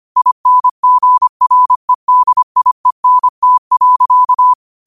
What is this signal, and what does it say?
Morse code, heard as a keyed tone, about 25 wpm, 1000 Hz. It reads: INGREDIENT.
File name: qotdfiles / 25wpm